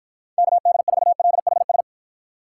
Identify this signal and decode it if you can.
Morse code, heard as a keyed tone, about 52 wpm, 700 Hz. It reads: KD4LHS